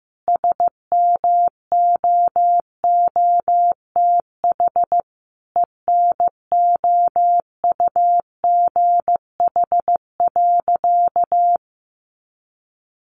Morse code, heard as a keyed tone, about 15 wpm, 700 Hz. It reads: SMOOTH ENOUGH.
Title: Sentences from Top 1400 Words 15wpm